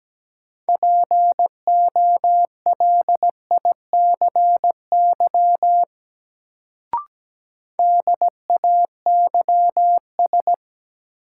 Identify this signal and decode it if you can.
Morse code, heard as a keyed tone, about 17 wpm, 700 Hz. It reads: POLICY DAYS
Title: Sentences from Top 1000 Words - Review Part 5 - New Word Drill - 17wpm